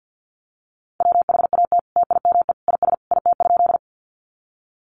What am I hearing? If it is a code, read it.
Morse code, heard as a keyed tone, about 50 wpm, 700 Hz. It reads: WHAT TIME IS IT?
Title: Sentences from Top 100 Words 50wpm